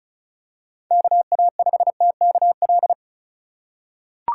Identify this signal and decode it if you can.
Morse code, heard as a keyed tone, about 36 wpm, 700 Hz. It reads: KA5TKL
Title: USA Call Signs - 6 Characters - 3x Character Spacing - 35wpm